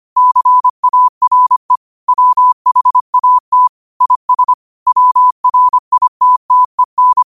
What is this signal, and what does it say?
Morse code, heard as a keyed tone, about 25 wpm, 1000 Hz. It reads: CARE WHAT IS WRITTEN
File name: qotdfiles / 25wpm